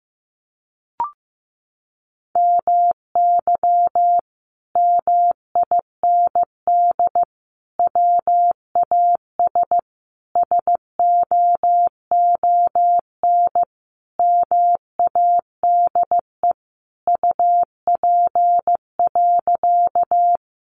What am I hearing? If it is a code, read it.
Morse code, heard as a keyed tone, about 15 wpm, 700 Hz. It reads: MY MIND WAS SOON MADE UP.
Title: Sentences from Top 500 Words 15wpm